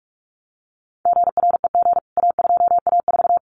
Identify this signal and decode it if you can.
Morse code, heard as a keyed tone, about 44 wpm, 700 Hz. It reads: GREG A2A4
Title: CWT Exchange 45wpm